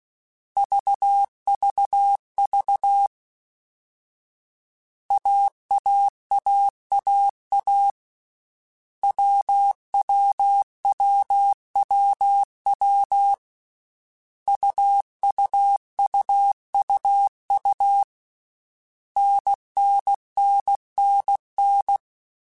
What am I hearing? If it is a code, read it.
Morse code, heard as a keyed tone, about 16 wpm, 800 Hz. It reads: VVV AAAAA WWWWW UUUUU NNNNN
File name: Basic Sets / Basic Letter Groups Collective7 / AWUNDG@13